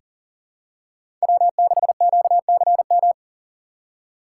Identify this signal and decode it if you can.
Morse code, heard as a keyed tone, about 41 wpm, 690 Hz. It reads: W6QCM
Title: USA Call Signs - 5 Characters - 3x Character Spacing - 40wpm